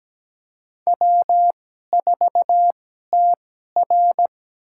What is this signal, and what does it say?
Morse code, heard as a keyed tone, about 17 wpm, spaced as for 12 wpm, 700 Hz. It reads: W4TR
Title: USA Call Signs - 4 Characters - 2x Character Spacing - 17wpm